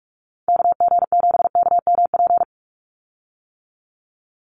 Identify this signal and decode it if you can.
Morse code, heard as a keyed tone, about 44 wpm, 700 Hz. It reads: KG7KMP
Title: USA Call Signs 45wpm